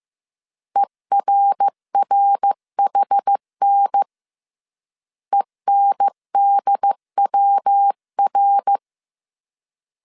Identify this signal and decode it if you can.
Morse code, heard as a keyed tone, about 15 wpm, 800 Hz. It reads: ERRHN ENDWR
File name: CW / DET / 1 3 1500C S C 14wpm 2458